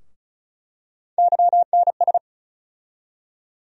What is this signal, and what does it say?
Morse code, heard as a keyed tone, about 34 wpm, 700 Hz. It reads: YNS
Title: USA Call Sign Postfixes - 3 Letters 35wpm